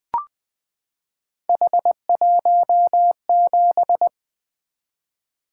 Morse code, heard as a keyed tone, about 20 wpm, 700 Hz. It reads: H17
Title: Three Letter-Number Combinations - 20wpm